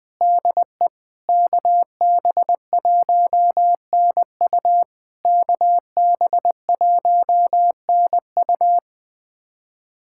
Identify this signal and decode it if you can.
Morse code, heard as a keyed tone, about 21 wpm, 690 Hz. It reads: DE KB1NU KB1NU
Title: SOTA QSO Practice 20wpm